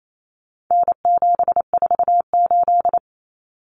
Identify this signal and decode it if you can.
Morse code, heard as a keyed tone, about 28 wpm, 700 Hz. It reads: N748
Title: Common Cut Numbers - 4-Digit 28wpm